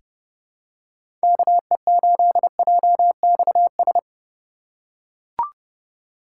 Morse code, heard as a keyed tone, about 30 wpm, 700 Hz. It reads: KE8JXS
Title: USA Call Signs - 6 Characters - 3x Character Spacing - 30wpm